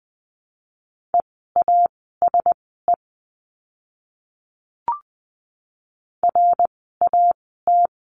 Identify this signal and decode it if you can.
Morse code, heard as a keyed tone, about 20 wpm, spaced as for 15 wpm, 700 Hz. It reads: EASE RAT
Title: 054 Lesson 10f - Letter r - Words 20wpm